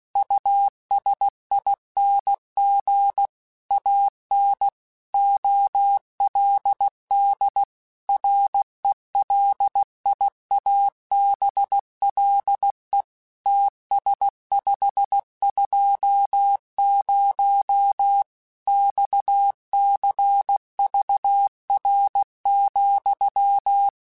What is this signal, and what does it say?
Morse code, heard as a keyed tone, about 16 wpm, 800 Hz. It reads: USING AN OLD RELIABLE TS520 XCVR,
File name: QSOs / QSOs-mp3s / WD0FBT 1@15